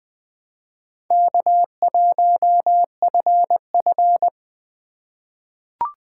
Morse code, heard as a keyed tone, about 20 wpm, 700 Hz. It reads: K1FF
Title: USA Call Signs - 3x Character Spacing - 20wpm